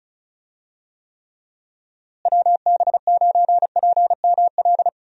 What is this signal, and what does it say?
Morse code, heard as a keyed tone, about 34 wpm, 700 Hz. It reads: WB9PML
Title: USA Call Signs - 6 Characters - Repeated 2x - 35wpm